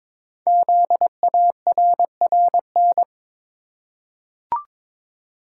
Morse code, heard as a keyed tone, about 23 wpm, 700 Hz. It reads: ZARRN
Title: Code Groups - 5 Letters - 22wpm